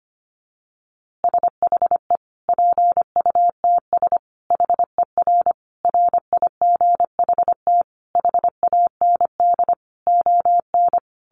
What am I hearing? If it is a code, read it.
Morse code, heard as a keyed tone, about 25 wpm, 700 Hz. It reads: SHE PUTS HER RIGHT HAND ON
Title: Easy Phrases 25wpm